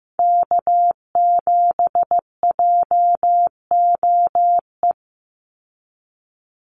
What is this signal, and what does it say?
Morse code, heard as a keyed tone, about 15 wpm, 700 Hz. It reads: K7JOE
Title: Call Signs Repeated 2x - 15wpm